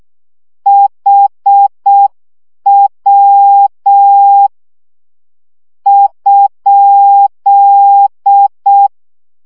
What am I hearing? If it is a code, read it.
Morse code, heard as a keyed tone, about 6 wpm, spaced as for 5 wpm, 800 Hz. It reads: HW?
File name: morze vizsga 2011-2 5